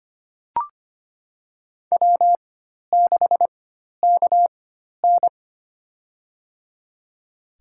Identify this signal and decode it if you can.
Morse code, heard as a keyed tone, about 25 wpm, spaced as for 12 wpm, 700 Hz. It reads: W6KN
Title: USA Call Signs - 4 Characters - 4x Character Spacing - 25wpm